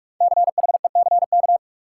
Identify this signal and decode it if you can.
Morse code, heard as a keyed tone, about 46 wpm, 700 Hz. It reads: CHECK
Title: Sentences from Top 1400 Words - New Word Drill Alt - 45wpm